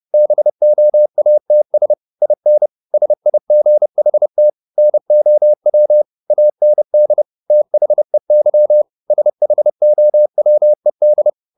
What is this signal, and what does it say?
Morse code, heard as a keyed tone, about 30 wpm, 600 Hz. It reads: BOATS IN SIGHT NOW AND THEY SHOWED
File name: the old man and the sea / the old man and the sea 1 30wpm